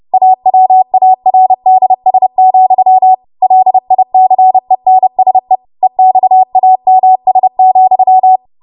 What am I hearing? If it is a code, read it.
Morse code, heard as a keyed tone, about 30 wpm, 750 Hz. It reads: AWARDS, LICENSE EXAMS,
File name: Morse / 30WPM